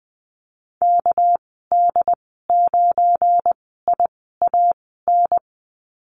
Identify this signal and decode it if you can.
Morse code, heard as a keyed tone, about 20 wpm, spaced as for 15 wpm, 700 Hz. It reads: KD9IAN
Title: 219 Lesson 35g - Number 0 - Call Signs 20wpm